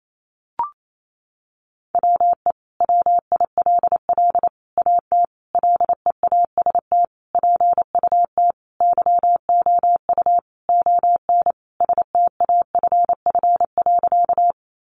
Morse code, heard as a keyed tone, about 28 wpm, 700 Hz. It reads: WE WILL AT LEAST PUT YOU ON STAFF.
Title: Sentences from Top 1200 Words 28wpm